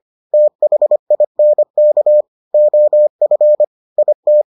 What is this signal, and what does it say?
Morse code, heard as a keyed tone, about 25 wpm, 600 Hz. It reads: THINK OF IT